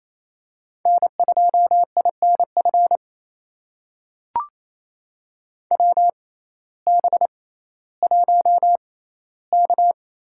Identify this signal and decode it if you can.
Morse code, heard as a keyed tone, about 28 wpm, spaced as for 15 wpm, 700 Hz. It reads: N2INF WB1K
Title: USA Call Signs - 6x Character Spacing - 28wpm